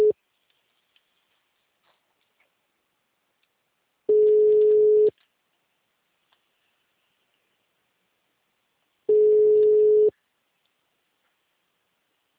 Ringback tone (Europe, 425 Hz).